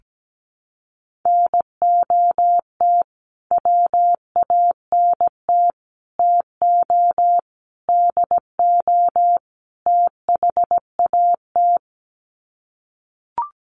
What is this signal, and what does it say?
Morse code, heard as a keyed tone, about 17 wpm, 700 Hz. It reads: NOT WANT TO DO THAT
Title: Sets of 5 Words Encore 17wpm